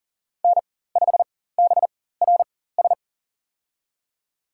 Morse code, heard as a keyed tone, about 40 wpm, spaced as for 19 wpm, 700 Hz. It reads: N5BRS
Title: USA Call Signs - 4x Character Spacing - 40wpm